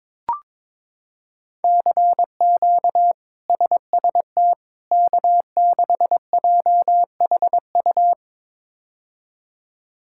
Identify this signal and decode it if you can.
Morse code, heard as a keyed tone, about 22 wpm, 700 Hz. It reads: CQ SST K6JHU
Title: K1USN SST QSO Practice - 22wpm